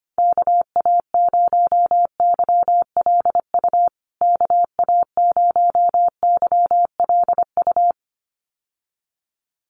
Morse code, heard as a keyed tone, about 25 wpm, 700 Hz. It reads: KA0YLU KA0YLU
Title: USA Call Signs Repeated 2x 25wpm